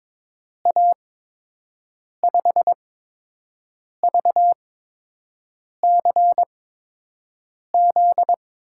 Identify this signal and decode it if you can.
Morse code, heard as a keyed tone, about 22 wpm, spaced as for 6 wpm, 700 Hz. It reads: A5VCZ
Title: USA Call Signs - 6 Characters - 8x Character Spacing - 22wpm